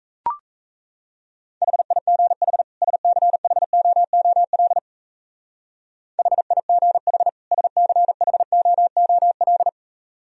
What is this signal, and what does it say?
Morse code, heard as a keyed tone, about 33 wpm, 700 Hz. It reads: HIGH SCHOOL HIGH SCHOOL
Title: Speed-Racing - Sets of 2 Words 28wpm